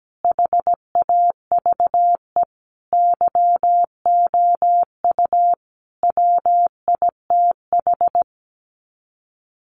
Morse code, heard as a keyed tone, about 17 wpm, 700 Hz. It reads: HAVE YOU WITH
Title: Sets of 5 Words Encore2 17wpm